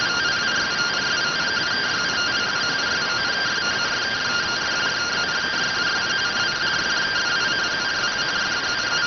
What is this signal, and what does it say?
Two-tone teleprinter signal, about 170 Hz shift.